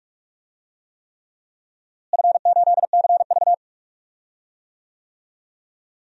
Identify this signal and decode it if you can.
Morse code, heard as a keyed tone, about 46 wpm, 700 Hz. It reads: F8CV
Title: DXCC-100 Call Signs - 4 Characters - Repeated 3x - 45wpm